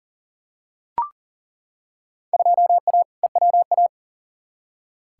Morse code, heard as a keyed tone, about 40 wpm, 700 Hz. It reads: 2A EWA
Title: Field Day Exchanges - 40wpm